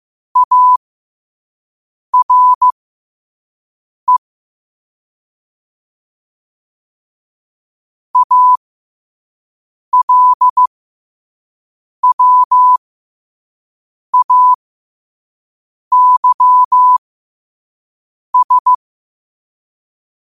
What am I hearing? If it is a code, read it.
Morse code, heard as a keyed tone, about 15 wpm, spaced as for 5 wpm, 1000 Hz. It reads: ARE ALWAYS
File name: qotdfiles / 5wpm